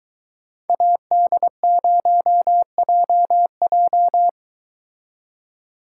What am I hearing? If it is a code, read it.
Morse code, heard as a keyed tone, about 23 wpm, 700 Hz. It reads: AD0JJ